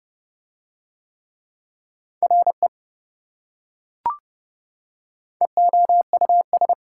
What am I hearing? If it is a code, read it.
Morse code, heard as a keyed tone, about 26 wpm, 700 Hz. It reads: RE EOUS